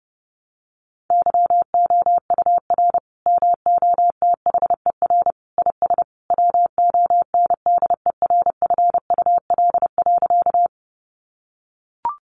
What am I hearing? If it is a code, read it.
Morse code, heard as a keyed tone, about 30 wpm, 700 Hz. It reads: YOUR MOTHER IS WONDERFUL.
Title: Sentences from Top 1100 Words 30wpm